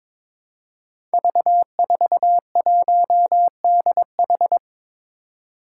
Morse code, heard as a keyed tone, about 22 wpm, 700 Hz. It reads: V41DH
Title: Five Challenging Letter-Number Combinations 22wpm